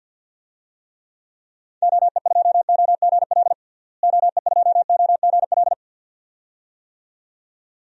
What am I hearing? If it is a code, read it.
Morse code, heard as a keyed tone, about 51 wpm, 700 Hz. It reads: OE2OGL OE2OGL
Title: DXCC-100 Call Signs - 6 Characters - Repeated 2x - 50wpm